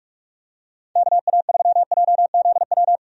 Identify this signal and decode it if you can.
Morse code, heard as a keyed tone, about 43 wpm, 700 Hz. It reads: KA3JZW
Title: USA Call Signs - 6 Characters - 45wpm